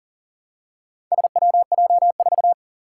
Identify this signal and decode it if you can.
Morse code, heard as a keyed tone, about 44 wpm, 700 Hz. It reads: SWJ4
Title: Four Challenging Letter-Number Combinations 40wpm